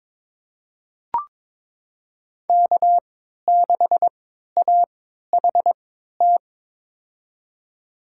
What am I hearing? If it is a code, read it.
Morse code, heard as a keyed tone, about 21 wpm, spaced as for 12 wpm, 700 Hz. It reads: K6AHT
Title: USA Call Signs - 5 Characters - 3x Character Spacing - 22wpm